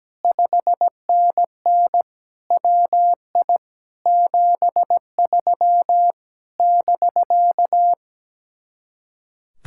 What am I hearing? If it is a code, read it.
Morse code, heard as a keyed tone, about 17 wpm, 700 Hz. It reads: 5NN WI 73 <BK>
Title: Speed-Racing Common QSO Phrases - 17wpm